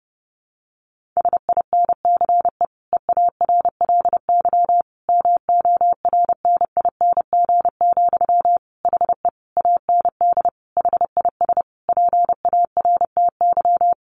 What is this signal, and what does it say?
Morse code, heard as a keyed tone, about 30 wpm, 700 Hz. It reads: SINCE EARLY MORNING, HE AND HIS PARTY
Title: Sentences from Top 1400 Words 30wpm